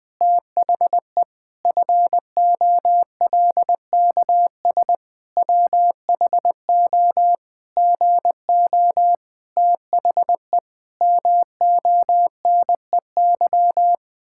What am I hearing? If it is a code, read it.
Morse code, heard as a keyed tone, about 20 wpm, 700 Hz. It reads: THE FOLKS WHO GO THE MONEY
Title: Pangram Challenge v2 - 20wpm